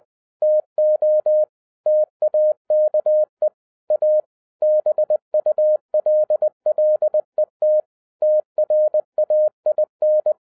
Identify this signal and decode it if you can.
Morse code, heard as a keyed tone, about 20 wpm, 610 Hz. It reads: TO TAKE A BULLET TRAIN